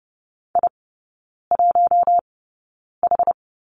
Morse code, heard as a keyed tone, about 30 wpm, spaced as for 9 wpm, 690 Hz. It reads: I1H
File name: Lesson 16e - Letter c - Three Characters 30wpm